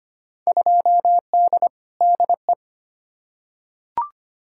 Morse code, heard as a keyed tone, about 25 wpm, 700 Hz. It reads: 2D DE
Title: Field Day Exchanges Encore - 25wpm